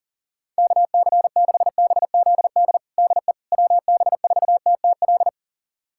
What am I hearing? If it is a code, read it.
Morse code, heard as a keyed tone, about 37 wpm, 700 Hz. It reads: KC6BZD DE WB4TTL
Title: USA Callsign DE Callsign - 6 Characters - 40wpm